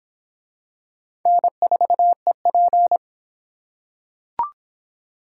Morse code, heard as a keyed tone, about 23 wpm, 700 Hz. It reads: N4EP